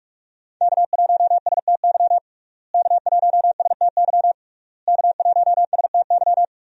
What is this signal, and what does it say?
Morse code, heard as a keyed tone, about 45 wpm, 700 Hz. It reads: K1STY K1STY K1STY
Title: USA Call Signs - 5 Characters - Repeated 3x - 45wpm